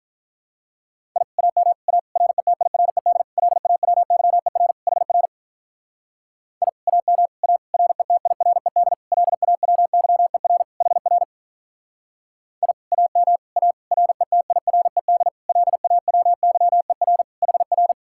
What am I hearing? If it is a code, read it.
Morse code, heard as a keyed tone, about 45 wpm, 700 Hz. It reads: I AM A RETIRED LAWYER HR I AM A RETIRED LAWYER HR I AM A RETIRED LAWYER HR